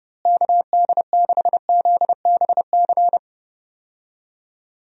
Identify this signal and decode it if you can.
Morse code, heard as a keyed tone, about 32 wpm, 700 Hz. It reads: KD6ZBC
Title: USA Call Signs - 6 Characters - Repeated 3x - 30wpm